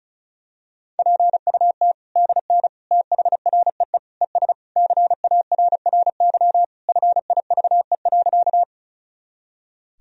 Morse code, heard as a keyed tone, about 36 wpm, 700 Hz. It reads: PUT DN THREE ES CARRY FIVE.
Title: Sentences from Top 400 Words and Abbreviations 35wpm